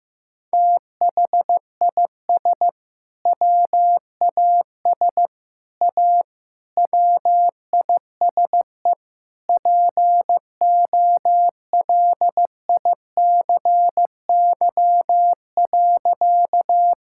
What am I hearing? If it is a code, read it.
Morse code, heard as a keyed tone, about 16 wpm, 700 Hz. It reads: THIS WAS A WISE POLICY.
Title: Sentences from Top 1600 Words 15wpm